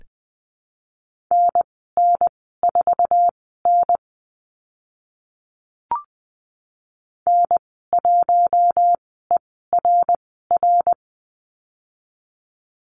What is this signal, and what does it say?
Morse code, heard as a keyed tone, about 20 wpm, spaced as for 15 wpm, 700 Hz. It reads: NN4N N1ERR
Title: 055 Lesson 10g - Letter r - Call Signs 20wpm